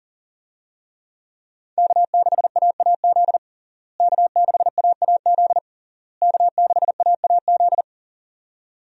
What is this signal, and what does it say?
Morse code, heard as a keyed tone, about 40 wpm, 700 Hz. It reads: K6AAZ K6AAZ K6AAZ
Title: USA Call Signs - 5 Characters - Repeated 3x - 40wpm